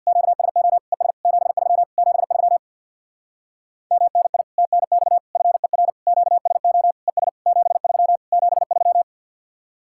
Morse code, heard as a keyed tone, about 65 wpm, 700 Hz. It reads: QSO ES 73 73 MNI TNX FER QSO ES 73 73